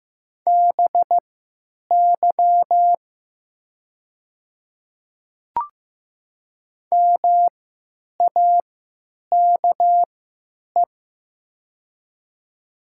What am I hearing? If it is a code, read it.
Morse code, heard as a keyed tone, about 15 wpm, spaced as for 9 wpm, 690 Hz. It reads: BY MAKE
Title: 3x Character Spacing - Top 100 Words - 15wpm